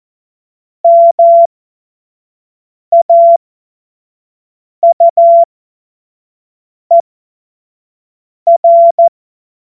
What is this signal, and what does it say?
Morse code, heard as a keyed tone, about 14 wpm, spaced as for 5 wpm, 680 Hz. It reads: MAUER
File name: Morsekurs / Witze / Witz 31. Mai 2022 25 F70